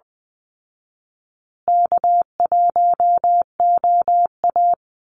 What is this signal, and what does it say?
Morse code, heard as a keyed tone, about 20 wpm, 700 Hz. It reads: K1OA
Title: Call Signs Repeated 2x - 20wpm